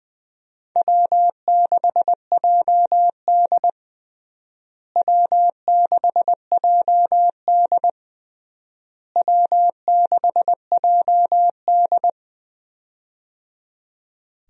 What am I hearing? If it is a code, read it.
Morse code, heard as a keyed tone, about 21 wpm, 700 Hz. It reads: W6JD W6JD W6JD
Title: USA Call Signs - 4 Characters - Repeated 3x - 20wpm